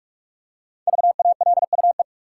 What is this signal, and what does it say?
Morse code, heard as a keyed tone, about 46 wpm, 700 Hz. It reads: VALUE
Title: Five Letter Words - 45wpm